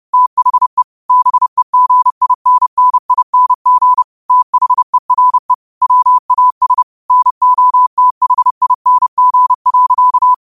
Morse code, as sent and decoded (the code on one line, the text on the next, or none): - .... / -... --. .. -. -. .. -. --. / - .... .-. / .-- .- ... / -. --- - .... .. -. --. .-.-.-
THE BEGINNING THERE WAS NOTHING.